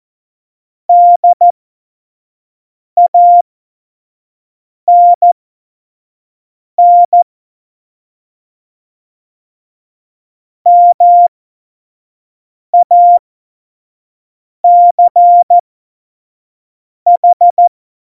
Morse code, heard as a keyed tone, about 14 wpm, spaced as for 5 wpm, 700 Hz. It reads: DANN MACH
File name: Morsekurs / Witze / Witz 17.Juli 2018, 25 F70